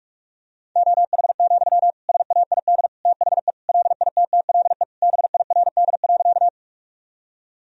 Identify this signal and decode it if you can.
Morse code, heard as a keyed tone, about 46 wpm, 700 Hz. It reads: OH, SAID THE LITTLE BIRD.
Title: Sentences from Top 1000 Words - Review Part 2 - 45wpm